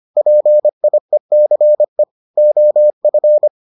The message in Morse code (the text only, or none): PIECE OF